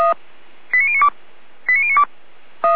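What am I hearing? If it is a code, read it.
Five-tone selective call (ZVEI), heard as quick runs of five tones, tones of about 70 ms. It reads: 89092 89092
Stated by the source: ZVEI five-tone selective call received on an SDR.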